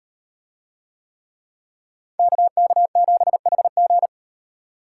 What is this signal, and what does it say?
Morse code, heard as a keyed tone, about 38 wpm, 700 Hz. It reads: KK7HG